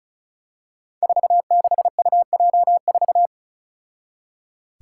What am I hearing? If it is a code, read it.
Morse code, heard as a keyed tone, about 36 wpm, 700 Hz. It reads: 46UJ4
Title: Five Challenging Letter-Number Combinations 35wpm